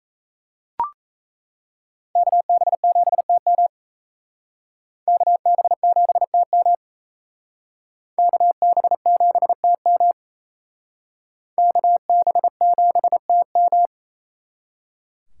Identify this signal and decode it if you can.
Morse code, heard as a keyed tone, about 35 wpm, 700 Hz. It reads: KB7TM KB7TM KB7TM KB7TM
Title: Speed Racing - USA Call Signs 28wpm